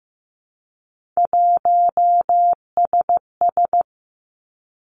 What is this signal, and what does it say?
Morse code, heard as a keyed tone, about 15 wpm, 700 Hz. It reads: 1SS
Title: Three Challenging Letter-Number Combinations 15wpm